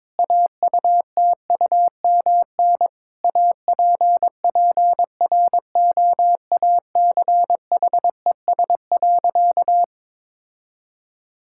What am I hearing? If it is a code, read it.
Morse code, heard as a keyed tone, about 21 wpm, 700 Hz. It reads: AUTUMN APPROACHES.
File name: Difficult Sentences with Hints - 22wpm